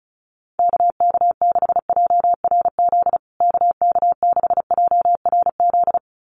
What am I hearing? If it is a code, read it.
Morse code, heard as a keyed tone, about 34 wpm, 700 Hz. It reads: KK6JRZ KK6JRZ
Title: USA Call Signs Repeated 2x 35wpm